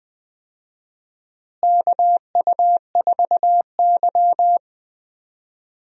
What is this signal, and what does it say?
Morse code, heard as a keyed tone, about 20 wpm, 700 Hz. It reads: KU4Y
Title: USA Call Signs - 4 Characters - Repeated 2x - 20wpm